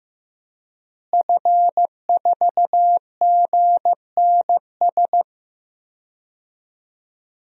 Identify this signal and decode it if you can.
Morse code, heard as a keyed tone, about 16 wpm, 700 Hz. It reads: F4GNS
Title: DXCC-100 Call Signs - 5 Characters - Repeated 3x - 15wpm